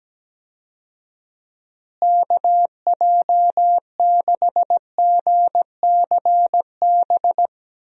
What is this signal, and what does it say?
Morse code, heard as a keyed tone, about 17 wpm, 700 Hz. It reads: KJ6GCB
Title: USA Call Signs - 6 Characters - Repeated 2x - 17wpm